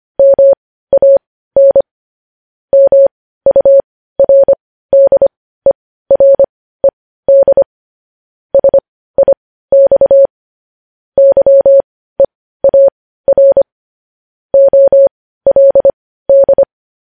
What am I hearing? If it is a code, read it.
Morse code, heard as a keyed tone, about 25 wpm, spaced as for 15 wpm, 550 Hz. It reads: MAN MURDERED SIX YEAR OLD
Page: news-headlines-in-morse-code-at-15-wpm